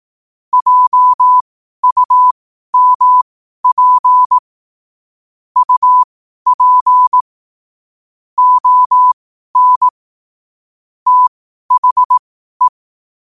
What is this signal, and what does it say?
Morse code, heard as a keyed tone, about 18 wpm, spaced as for 12 wpm, 1000 Hz. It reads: JUMP UP ON THE
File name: qotdfiles / 12wpm